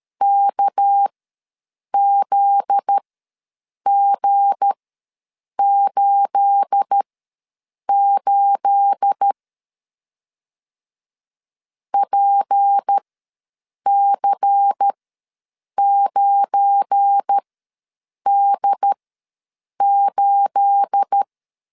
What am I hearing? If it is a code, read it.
Morse code, heard as a keyed tone, about 13 wpm, spaced as for 7 wpm, 800 Hz. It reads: KZG88 PC9D8
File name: LEZIONI / 9 28 ws 12wpm 3354